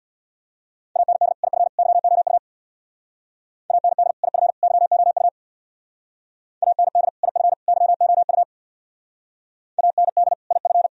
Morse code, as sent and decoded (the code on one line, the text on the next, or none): .- -. -.. / .. ..-. / -.-- --- ..- / .- -. -.. / .. ..-. / -.-- --- ..- / .- -. -.. / .. ..-. / -.-- --- ..- / .- -. -.. / .. ..-.
AND IF YOU AND IF YOU AND IF YOU AND IF